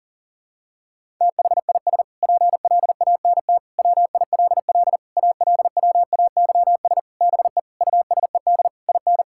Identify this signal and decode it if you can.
Morse code, heard as a keyed tone, about 40 wpm, 700 Hz. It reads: THIS PLANT WILL ALWAYS BE USED IN
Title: Sentences from Top 1000 Words - Review Part 6 - 40wpm